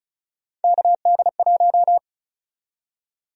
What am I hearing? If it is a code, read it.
Morse code, heard as a keyed tone, about 36 wpm, 700 Hz. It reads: KD1
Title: USA Call Sign Prefixes - Repeated 2x - 35wpm